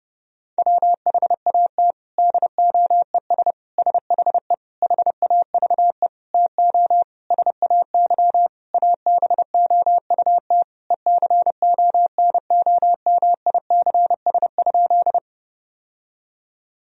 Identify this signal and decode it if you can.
Morse code, heard as a keyed tone, about 30 wpm, 700 Hz. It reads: WHAT DOES SHE HAVE TO SAY ABOUT ECONOMICS?
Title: Sentences from Top 1000 Words - Review Part 3 - 30wpm